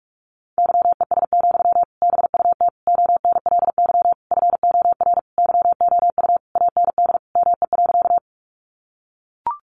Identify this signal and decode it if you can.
Morse code, heard as a keyed tone, about 44 wpm, 700 Hz. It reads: YES, BUT ONLY FOR YOU AND ME.
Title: Sentences from Top 500 Words 45wpm